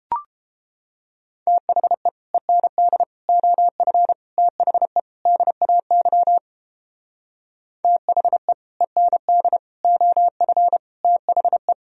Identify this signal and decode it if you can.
Morse code, heard as a keyed tone, about 29 wpm, 700 Hz. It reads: THE END OF THE DAY THE END OF THE